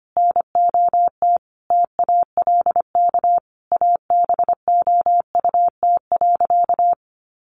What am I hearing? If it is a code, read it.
Morse code, heard as a keyed tone, about 25 wpm, 700 Hz. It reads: NOT TALK ABOUT.
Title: Sentences from Top 800 Words 25wpm